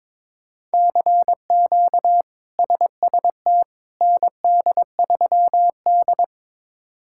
Morse code, heard as a keyed tone, about 22 wpm, 700 Hz. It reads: CQ SST ND3D
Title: K1USN SST QSO Practice - 22wpm